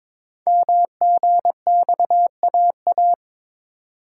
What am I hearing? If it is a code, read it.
Morse code, heard as a keyed tone, about 22 wpm, 700 Hz. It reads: MGXAA
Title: Code Groups - 5 Letters - 22wpm